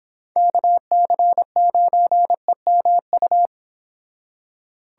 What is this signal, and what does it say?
Morse code, heard as a keyed tone, about 26 wpm, 700 Hz. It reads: KC9EMU